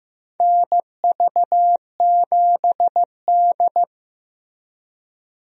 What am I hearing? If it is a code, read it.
Morse code, heard as a keyed tone, about 15 wpm, 700 Hz. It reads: NV7D
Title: USA Call Signs - 4 Characters - 15wpm